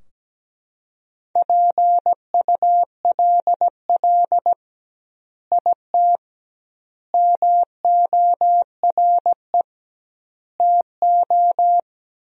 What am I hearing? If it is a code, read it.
Morse code, heard as a keyed tone, about 17 wpm, 700 Hz. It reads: PULL IT MORE TO
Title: 2x Word-Spacing - Sentences from Top 400 Words - 17wpm